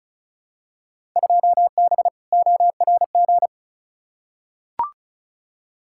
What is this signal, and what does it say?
Morse code, heard as a keyed tone, about 36 wpm, 690 Hz. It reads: 2B ORG
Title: Field Day Exchanges Encore - 35wpm